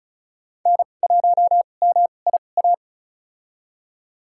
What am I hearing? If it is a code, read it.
Morse code, heard as a keyed tone, about 34 wpm, spaced as for 25 wpm, 700 Hz. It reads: N1MIA
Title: USA Call Signs - 2x Character Spacing - 35wpm